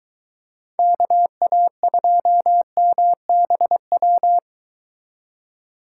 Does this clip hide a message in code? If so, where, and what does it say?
Yes: Morse code from 0.8 to 4.4 s, heard as a keyed tone, about 22 wpm, 700 Hz. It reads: KA2MBW